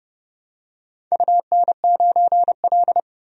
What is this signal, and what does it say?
Morse code, heard as a keyed tone, about 30 wpm, 700 Hz. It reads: UN9L
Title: DXCC-100 Call Signs - 4 Characters - 30wpm